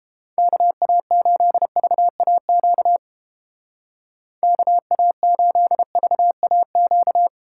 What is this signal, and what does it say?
Morse code, heard as a keyed tone, about 30 wpm, 700 Hz. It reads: KA8VAQ KA8VAQ